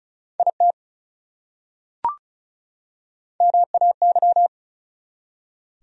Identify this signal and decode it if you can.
Morse code, heard as a keyed tone, about 34 wpm, 700 Hz. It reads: IT MAY
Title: Top 100 Words 35wpm